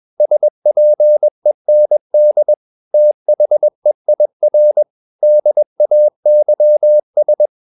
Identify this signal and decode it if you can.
Morse code, heard as a keyed tone, about 21 wpm, 600 Hz. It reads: SPEND THEIR DAYS
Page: CW med Gnister